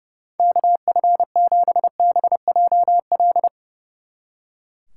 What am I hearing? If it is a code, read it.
Morse code, heard as a keyed tone, about 36 wpm, 700 Hz. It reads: KF7BJL
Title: Speed Racing - USA 2x3 Call Signs 30wpm